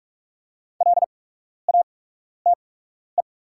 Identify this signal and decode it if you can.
Morse code, heard as a keyed tone, about 45 wpm, spaced as for 12 wpm, 690 Hz. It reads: LATE
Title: 8x Character Spacing - Top 300 Words - 45wpm